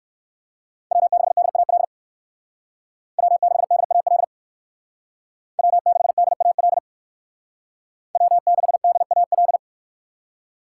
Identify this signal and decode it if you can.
Morse code, heard as a keyed tone, about 56 wpm, 700 Hz. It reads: W6DAL W6DAL W6DAL W6DAL
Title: Speed Racing - USA 1x3 Call Signs 45wpm